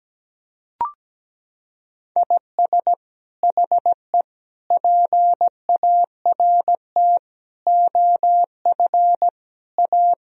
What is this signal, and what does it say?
Morse code, heard as a keyed tone, about 17 wpm, 700 Hz. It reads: IS HE PART OF A
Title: Sentences from Top 1000 Words - Review Part 4 - 17wpm